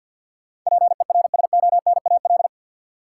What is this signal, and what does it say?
Morse code, heard as a keyed tone, about 52 wpm, 700 Hz. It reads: PERSONAL
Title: Sentences from Top 700 Words - New Word Drill - 50wpm